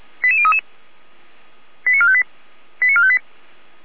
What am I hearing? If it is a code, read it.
Five-tone selective call (ZVEI), heard as quick runs of five tones, tones of about 70 ms. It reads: 88033 89458 89458
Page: /455/SDR/ZVEI/89143/